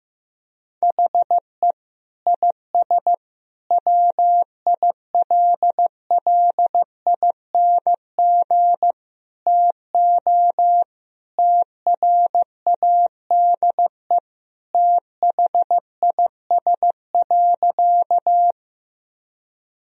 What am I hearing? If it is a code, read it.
Morse code, heard as a keyed tone, about 15 wpm, 700 Hz. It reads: HE IS WILLING TO TRADE THIS.
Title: Sentences from Top 1000 Words - Review Part 4 - 15wpm